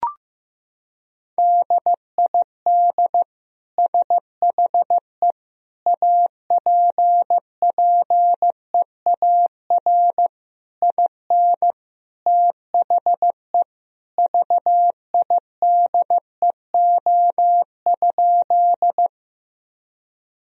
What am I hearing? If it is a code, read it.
Morse code, heard as a keyed tone, about 16 wpm, 700 Hz. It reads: DID SHE APPEAR IN THE VIDEO?
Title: Sentences from Top 1000 Words - Review Part 4 - 15wpm